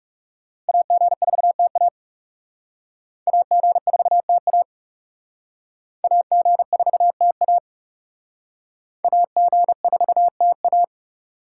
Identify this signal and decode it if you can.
Morse code, heard as a keyed tone, about 38 wpm, 700 Hz. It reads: AG4TA AG4TA AG4TA AG4TA